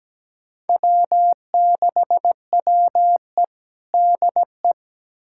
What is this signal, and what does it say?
Morse code, heard as a keyed tone, about 17 wpm, 700 Hz. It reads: W6WE DE
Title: USA Callsign DE Callsign - 4 Characters - 17wpm